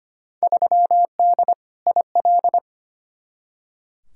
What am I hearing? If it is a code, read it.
Morse code, heard as a keyed tone, about 25 wpm, 700 Hz. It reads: 3D IL
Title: Field Day Exchanges Encore - 25wpm